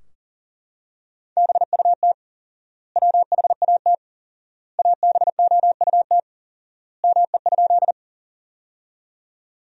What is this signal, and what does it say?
Morse code, heard as a keyed tone, about 37 wpm, 700 Hz. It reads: BUT WHAT ABOUT ME?
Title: 4x Word-Spacing - Sentences from Top 200 Words - 40wpm